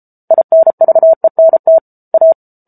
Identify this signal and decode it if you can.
Morse code, heard as a keyed tone, about 34 wpm, 650 Hz. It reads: INVENT A